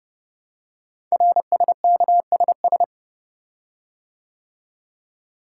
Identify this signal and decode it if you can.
Morse code, heard as a keyed tone, about 30 wpm, 700 Hz. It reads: RSKSS
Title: Code Groups - 5 Letters - 30wpm